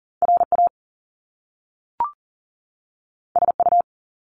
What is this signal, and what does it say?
Morse code, heard as a keyed tone, about 40 wpm, 700 Hz. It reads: RA SU
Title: Top Two Letter Combinations in Words 40wpm